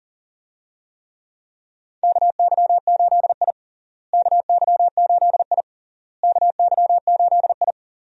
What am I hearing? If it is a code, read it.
Morse code, heard as a keyed tone, about 40 wpm, 700 Hz. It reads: KY8I KY8I KY8I
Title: USA Call Signs - 4 Characters - Repeated 3x - 40wpm